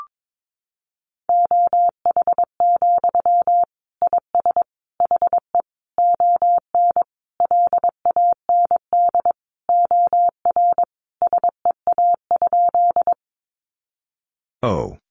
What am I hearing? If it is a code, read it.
Morse code, heard as a keyed tone, about 22 wpm, 700 Hz. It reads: OH, IS HE ON LAND OR SEA?
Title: Sentences from Top 500 Words 22wpm